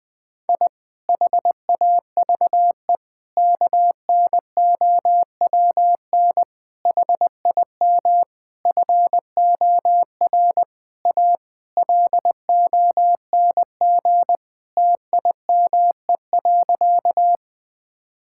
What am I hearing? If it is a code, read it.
Morse code, heard as a keyed tone, about 20 wpm, 700 Hz. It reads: I HAVE KNOWN HIM FOR A LONG TIME.
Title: Sentences with 4-Word Phrases - Spoken Beginning - 20wpm